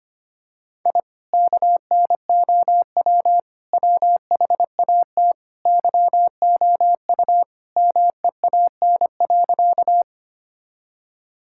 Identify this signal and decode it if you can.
Morse code, heard as a keyed tone, about 25 wpm, 700 Hz. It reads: I KNOW WHAT YOU MEAN.
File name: practice / 25wpm / Sentences with 2-Word Phrases - 25wpm